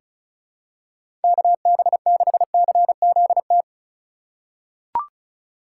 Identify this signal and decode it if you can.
Morse code, heard as a keyed tone, about 36 wpm, 700 Hz. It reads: KB6CZT
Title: USA Call Signs - 6 Characters - 4x Character Spacing - 35wpm